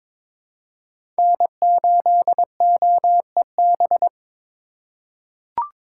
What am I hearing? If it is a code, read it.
Morse code, heard as a keyed tone, about 22 wpm, 700 Hz. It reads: N8OEB
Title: USA Call Signs - 4x Character Spacing - 22wpm